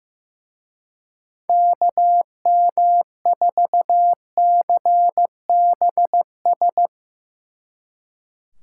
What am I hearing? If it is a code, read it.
Morse code, heard as a keyed tone, about 16 wpm, 700 Hz. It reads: KM4CBS
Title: Speed Racing - USA 2x3 Call Signs 15wpm